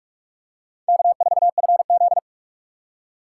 Morse code, heard as a keyed tone, about 42 wpm, 700 Hz. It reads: K4FZ